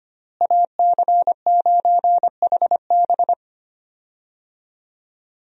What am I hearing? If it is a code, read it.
Morse code, heard as a keyed tone, about 25 wpm, 700 Hz. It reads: AC9HB
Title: USA Call Signs - 5 Characters - Repeated 3x - 25wpm